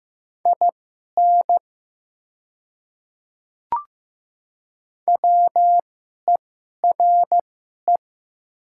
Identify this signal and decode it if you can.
Morse code, heard as a keyed tone, about 15 wpm, spaced as for 11 wpm, 700 Hz. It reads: IN WERE
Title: 2x Character Spacing - Top 100 Words - 15wpm